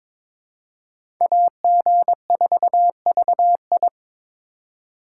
Morse code, heard as a keyed tone, about 22 wpm, 700 Hz. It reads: AG4VI
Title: USA Call Signs - 5 Characters - 2x Character Spacing - 22wpm